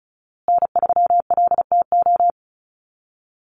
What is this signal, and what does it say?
Morse code, heard as a keyed tone, about 34 wpm, 700 Hz. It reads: N3LTO
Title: USA Call Signs Repeated 3x 35wpm